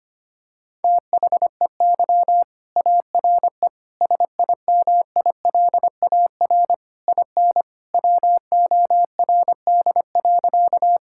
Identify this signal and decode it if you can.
Morse code, heard as a keyed tone, about 25 wpm, 700 Hz. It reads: THEY ARE SIMILAR IN WORD.
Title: Sentences from Top 1000 Words - Review Part 5 - 25wpm